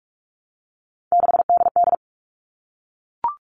Morse code, heard as a keyed tone, about 44 wpm, 700 Hz. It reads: 6DD
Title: All Cut Numbers - 3-Digit 45wpm